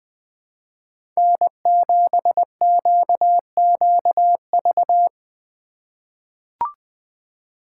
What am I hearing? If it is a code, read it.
Morse code, heard as a keyed tone, about 21 wpm, 700 Hz. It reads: N7QQV
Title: USA Call Signs - 5 Characters - 3x Character Spacing - 20wpm